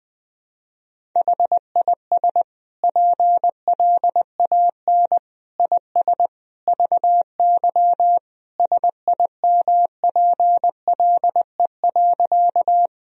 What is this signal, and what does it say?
Morse code, heard as a keyed tone, about 20 wpm, 700 Hz. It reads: HIS PLAN IS VY SIMPLE.
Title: Sentences from Top 500 Words and Abbreviations 20wpm